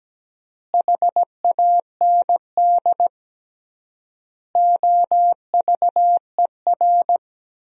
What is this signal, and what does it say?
Morse code, heard as a keyed tone, about 17 wpm, 700 Hz. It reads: HAND OVER
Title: 3x Word-Spacing - Sentences from Top 400 Words - 17wpm